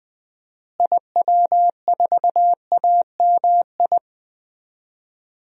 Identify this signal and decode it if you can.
Morse code, heard as a keyed tone, about 20 wpm, 700 Hz. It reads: IW4AMI
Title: DXCC-100 Call Signs - 6 Characters - Repeated 3x - 20wpm